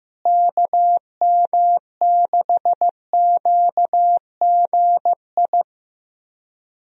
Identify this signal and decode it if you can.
Morse code, heard as a keyed tone, about 15 wpm, 700 Hz. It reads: KM6QGI
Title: USA Call Signs - 6 Characters - Repeated 2x - 15wpm